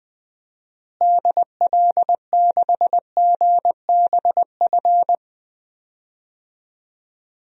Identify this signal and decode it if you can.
Morse code, heard as a keyed tone, about 20 wpm, 700 Hz. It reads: DL6GBF
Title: DXCC-100 Call Signs - 6 Characters - Repeated 2x - 20wpm